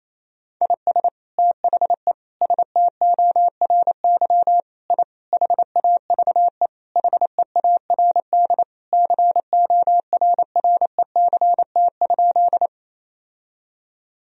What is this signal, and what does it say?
Morse code, heard as a keyed tone, about 28 wpm, 700 Hz. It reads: IS THE STORY I HAVE HEARD CORRECT?
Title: Sentences from Top 1000 Words - Review Part 2 - 28wpm